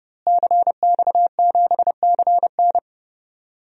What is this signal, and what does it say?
Morse code, heard as a keyed tone, about 30 wpm, 700 Hz. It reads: CX7CN
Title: DXCC-100 Call Signs - 5 Characters - Repeated 2x - 30wpm